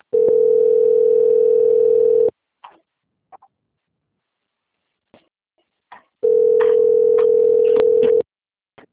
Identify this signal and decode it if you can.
Ringback tone (North America), 440 + 480 Hz.
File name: out- - - - - .67460